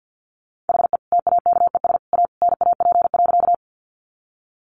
Morse code, heard as a keyed tone, about 50 wpm, 700 Hz. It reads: HE TAKES A NAP.
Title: Easy Phrases 50wpm